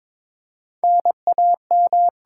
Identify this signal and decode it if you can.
Morse code, heard as a keyed tone, about 22 wpm, 690 Hz. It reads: NAM
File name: Top 90 QSO Elements - 8x Character Spacing - 22wpm